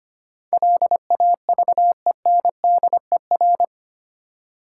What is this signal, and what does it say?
Morse code, heard as a keyed tone, about 25 wpm, 700 Hz. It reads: LAVENDER
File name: practice / 25wpm / Difficult Words with Hints - 25wpm